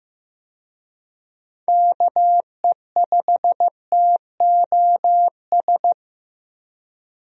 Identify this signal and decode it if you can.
Morse code, heard as a keyed tone, about 16 wpm, 700 Hz. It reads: KE5TOS
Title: USA Call Signs - 6 Characters - Repeated 2x - 15wpm